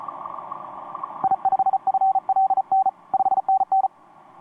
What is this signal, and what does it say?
Morse code, heard as a keyed tone, about 34 wpm, 770 Hz. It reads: I5FLN 5NN
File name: t46rrc i5fln etc 15m cw 18feb2013 1248